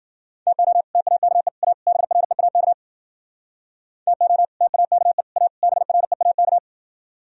Morse code, heard as a keyed tone, about 52 wpm, 700 Hz. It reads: TO TAKE A BREAK TO TAKE A BREAK